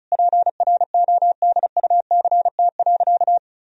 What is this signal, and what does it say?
Morse code, heard as a keyed tone, about 34 wpm, 700 Hz. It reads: PRODUCT.
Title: Sentences from Top 1000 Words - Review Part 2 - 35wpm